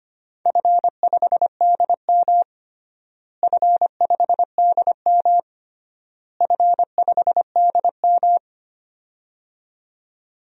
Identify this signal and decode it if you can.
Morse code, heard as a keyed tone, about 25 wpm, 700 Hz. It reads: F5DM F5DM F5DM
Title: DXCC-100 Call Signs - 4 Characters - Repeated 3x - 25wpm